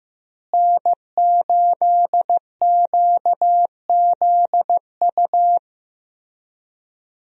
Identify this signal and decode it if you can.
Morse code, heard as a keyed tone, about 16 wpm, 700 Hz. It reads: N8QZU
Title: USA Call Signs - 5 Characters - Repeated 3x - 15wpm